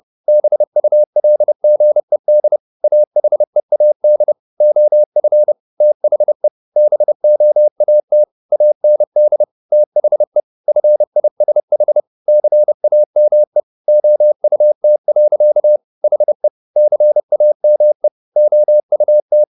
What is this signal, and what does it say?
Morse code, heard as a keyed tone, about 30 wpm, 600 Hz. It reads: BULGED AHEAD OF THE BOAT AND THE FISH CAME OUT. HE CAME OUT
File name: the old man and the sea / the old man and the sea 1 30wpm